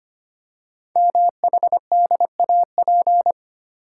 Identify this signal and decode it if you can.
Morse code, heard as a keyed tone, about 25 wpm, 700 Hz. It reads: MHDAP
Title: Code Groups - 5 Letters - 25wpm